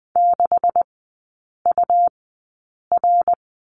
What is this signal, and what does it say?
Morse code, heard as a keyed tone, about 20 wpm, spaced as for 8 wpm, 700 Hz. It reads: 6UR
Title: 127 Lesson 21e - Number 6 - Three Characters 20wpm